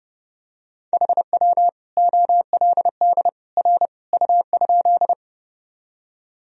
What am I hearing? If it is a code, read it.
Morse code, heard as a keyed tone, about 32 wpm, 700 Hz. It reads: HW OLD R U?
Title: Sentences from Top 200 Words and Abbreviations 30wpm